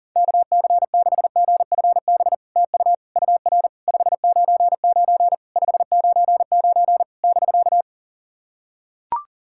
Morse code, heard as a keyed tone, about 42 wpm, 700 Hz. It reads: KC6GFB TU UR 599 599 <BK>
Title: SOTA QSO Practice 40wpm